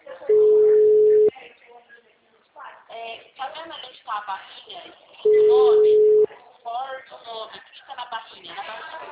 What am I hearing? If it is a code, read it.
Ringback tone (Europe, 425 Hz), with speech.
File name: out- -2017